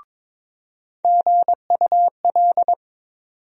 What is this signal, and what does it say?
Morse code, heard as a keyed tone, about 22 wpm, 700 Hz. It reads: GUL
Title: USA Call Sign Postfixes - 3 Letters 22wpm